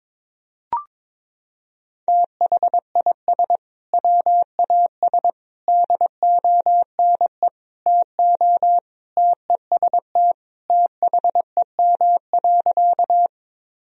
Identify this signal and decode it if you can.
Morse code, heard as a keyed tone, about 21 wpm, 700 Hz. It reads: THIS WAS DONE TO TEST THEM.
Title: Sentences from Top 1000 Words - Review Part 3 - 22wpm